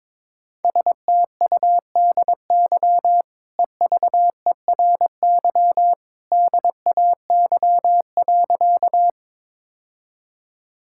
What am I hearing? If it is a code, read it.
Morse code, heard as a keyed tone, about 21 wpm, 700 Hz. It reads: STUDY EVERY DAY.
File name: practice / 22wpm / Easy Sentences with Hints - 22wpm